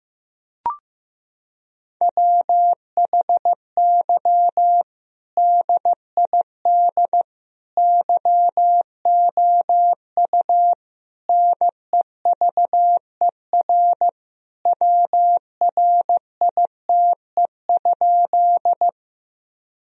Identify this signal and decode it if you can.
Morse code, heard as a keyed tone, about 15 wpm, 700 Hz. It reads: WHY DID YOU NEVER WRITE?
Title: Sentences from Top 1000 Words - Review Part 1 - 15wpm